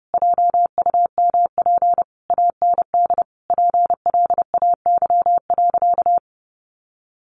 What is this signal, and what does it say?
Morse code, heard as a keyed tone, about 30 wpm, 700 Hz. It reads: JUMP AND PLAY.
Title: Sentences from Top 1300 Words 30wpm